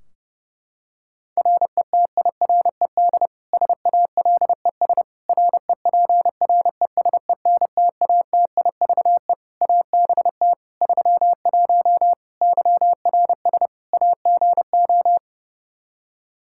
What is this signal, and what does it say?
Morse code, heard as a keyed tone, about 30 wpm, 700 Hz. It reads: RETIRED SALES REPRESENTATIVE ABT 31 YRS AGO
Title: Practice QSO - Exchange 3 - More Info - 30wpm